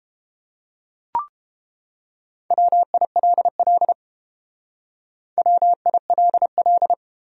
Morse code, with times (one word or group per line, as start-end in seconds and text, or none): 2.5-3.9 WILL
5.4-6.9 WILL